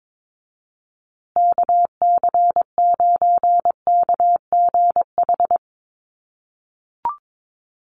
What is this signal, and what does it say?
Morse code, heard as a keyed tone, about 22 wpm, 700 Hz. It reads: KC9KGH
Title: USA Call Signs Repeated 3x 22wpm